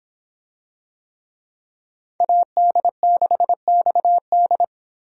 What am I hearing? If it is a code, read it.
Morse code, heard as a keyed tone, about 26 wpm, 700 Hz. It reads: AD6XD